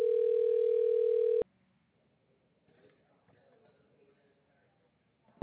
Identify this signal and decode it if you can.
Ringback tone (North America), 440 + 480 Hz.